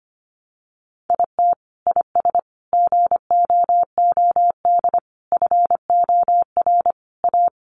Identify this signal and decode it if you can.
Morse code, heard as a keyed tone, about 25 wpm, 700 Hz. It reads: IT IS GOOD FOR A
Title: Sentences from Top 1400 Words 25wpm